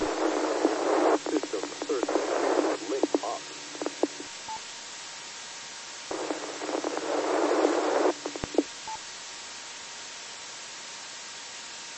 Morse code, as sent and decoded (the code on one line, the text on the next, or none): none